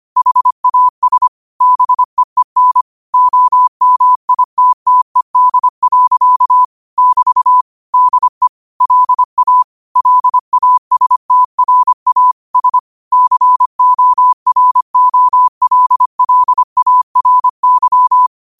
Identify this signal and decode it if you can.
Morse code, heard as a keyed tone, about 25 wpm, 1000 Hz. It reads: HAS BEEN OMITTED. <BT> DE LA LASTRA S COROLLARY
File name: qotdfiles / 25wpm